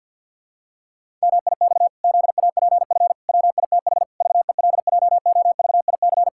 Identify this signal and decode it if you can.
Morse code, heard as a keyed tone, about 51 wpm, 700 Hz. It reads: MIX ZAPF WITH VELJOVIC
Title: Pangram Challenge v2 - 50wpm